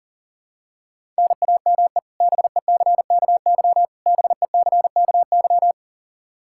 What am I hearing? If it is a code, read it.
Morse code, heard as a keyed tone, about 38 wpm, 700 Hz. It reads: NAME BECKY BECKY
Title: K1USN SST QSO Practice 40wpm